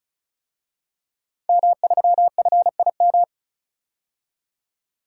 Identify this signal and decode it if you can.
Morse code, heard as a keyed tone, about 36 wpm, 700 Hz. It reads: M3FIM
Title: DXCC-100 Call Signs - 5 Characters - 35wpm